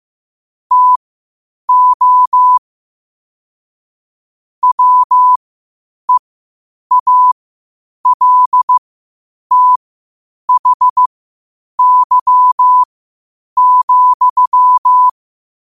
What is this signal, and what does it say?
Morse code, heard as a keyed tone, about 15 wpm, spaced as for 8 wpm, 1000 Hz. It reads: TO WEALTHY,
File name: qotdfiles / 8wpm